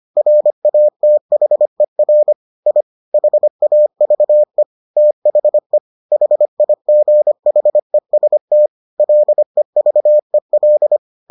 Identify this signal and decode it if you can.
Morse code, heard as a keyed tone, about 25 wpm, 600 Hz. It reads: RATHER I HAVE THE HIGHEST LEVEL